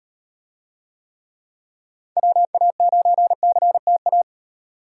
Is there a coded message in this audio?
Yes: Morse code from 2.2 to 4.2 s, heard as a keyed tone, about 38 wpm, 700 Hz. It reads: WA9CTA